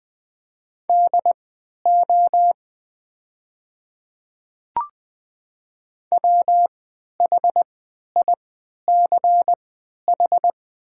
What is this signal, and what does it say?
Morse code, heard as a keyed tone, about 20 wpm, spaced as for 11 wpm, 690 Hz. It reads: DO WHICH
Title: 3x Character Spacing - Top 200 Words - 20wpm